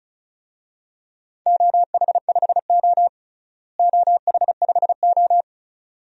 Morse code, heard as a keyed tone, about 36 wpm, 700 Hz. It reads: OH5O OH5O
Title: DXCC-100 Call Signs - 4 Characters - Repeated 2x - 35wpm